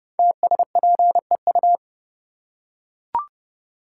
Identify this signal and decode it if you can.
Morse code, heard as a keyed tone, about 29 wpm, 700 Hz. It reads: TSPEU
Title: Code Groups - 5 Letter:Numbers - 30wpm